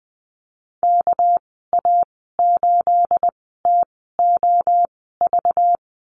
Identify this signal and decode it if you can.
Morse code, heard as a keyed tone, about 20 wpm, spaced as for 15 wpm, 700 Hz. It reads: KA8TOV
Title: 238 Lesson 38g - Letter q - Call Signs 20wpm